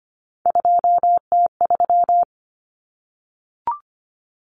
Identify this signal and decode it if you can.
Morse code, heard as a keyed tone, about 25 wpm, 700 Hz. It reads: 2T3
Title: Common Cut Numbers - 3-Digit 25wpm